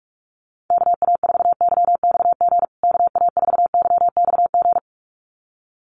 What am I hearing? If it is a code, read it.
Morse code, heard as a keyed tone, about 44 wpm, 700 Hz. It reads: KA4YXG KA4YXG
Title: USA Call Signs Repeated 2x 45wpm